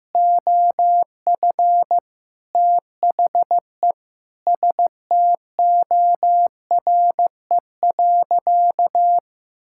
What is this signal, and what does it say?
Morse code, heard as a keyed tone, about 16 wpm, 700 Hz. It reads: OF THE STORE.
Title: Sentences from Top 1000 Words - Review Part 5 - 15wpm